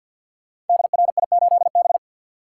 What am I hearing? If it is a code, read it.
Morse code, heard as a keyed tone, about 50 wpm, 700 Hz. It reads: DRI8B
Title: Code Groups - 5 Letter:Numbers - 50wpm